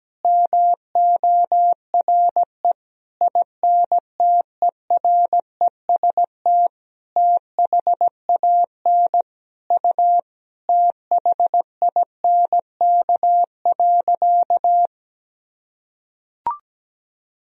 Morse code, heard as a keyed tone, about 17 wpm, 700 Hz. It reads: MORE INTEREST THAN U THINK.
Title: Sentences from Top 400 Words and Abbreviations 17wpm